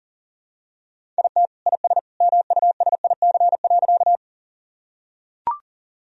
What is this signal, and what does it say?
Morse code, heard as a keyed tone, about 40 wpm, 700 Hz. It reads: IT IS MUSIC.
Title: Sentences from Top 1000 Words - Review Part 1 - 40wpm